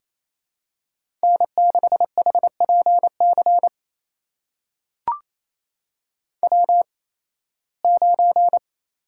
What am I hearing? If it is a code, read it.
Morse code, heard as a keyed tone, about 28 wpm, spaced as for 18 wpm, 700 Hz. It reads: N6HPC W9
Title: USA Call Signs - 5 Characters - 8x Character Spacing - 28wpm